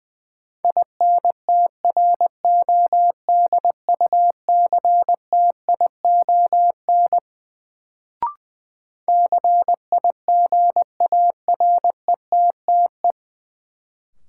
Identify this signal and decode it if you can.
Morse code, heard as a keyed tone, about 20 wpm, 700 Hz. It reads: INTRODUCTION CIGARETTE
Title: Sentences from Top 1000 Words - New Word Drill - 20wpm